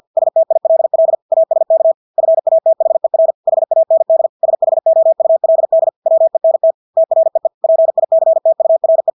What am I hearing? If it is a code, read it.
Morse code, heard as a keyed tone, about 50 wpm, 650 Hz. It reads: STILL ASK FATHER HAND SHOULD WENT TREE PICTURE
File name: Engelske ord 50 wpm